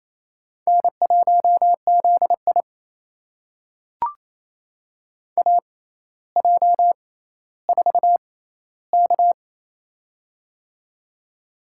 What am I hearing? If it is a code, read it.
Morse code, heard as a keyed tone, about 28 wpm, spaced as for 14 wpm, 700 Hz. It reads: N1ZI AJ4K
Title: USA Call Signs - 4 Characters - 6x Character Spacing - 28wpm